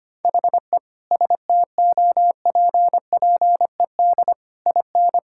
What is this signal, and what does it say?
Morse code, heard as a keyed tone, about 25 wpm, 700 Hz. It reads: HE STOPPED IN
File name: practice / 25wpm / Sentences with 3-Word Phrases - 25wpm